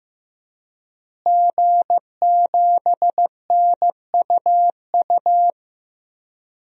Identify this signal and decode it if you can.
Morse code, heard as a keyed tone, about 15 wpm, 700 Hz. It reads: G7NUU
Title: DXCC-100 Call Signs - 5 Characters - Repeated 2x - 15wpm